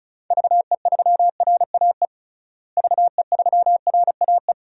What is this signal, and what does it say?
Morse code, heard as a keyed tone, about 36 wpm, 700 Hz. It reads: VE3RAE VE3RAE
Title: DXCC-100 Call Signs - 6 Characters - Repeated 2x - 35wpm